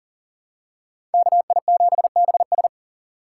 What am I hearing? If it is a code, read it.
Morse code, heard as a keyed tone, about 39 wpm, 700 Hz. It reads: KI7BS
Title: USA Call Signs - 5 Characters - 4x Character Spacing - 40wpm